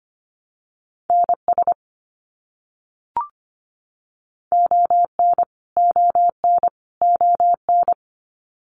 Morse code, heard as a keyed tone, about 25 wpm, 700 Hz. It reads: NS ON ON ON
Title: Canadian Provinces and Territories - Repeated 3x - 25wpm